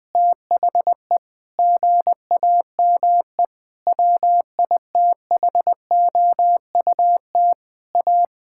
Morse code, heard as a keyed tone, about 20 wpm, 700 Hz. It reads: THE GAME WITHOUT A
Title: Sentences from Top 1000 Words - Review Part 6 - 20wpm